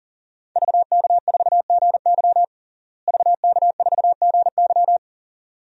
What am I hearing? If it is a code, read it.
Morse code, heard as a keyed tone, about 37 wpm, 700 Hz. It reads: VK4GY VK4GY
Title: DXCC-100 Call Signs - 5 Characters - Repeated 2x - 40wpm